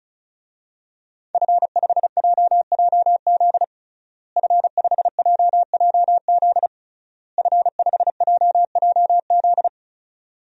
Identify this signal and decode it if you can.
Morse code, heard as a keyed tone, about 36 wpm, 700 Hz. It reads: F5JJZ F5JJZ F5JJZ
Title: DXCC-100 Call Signs - 5 Characters - Repeated 3x - 35wpm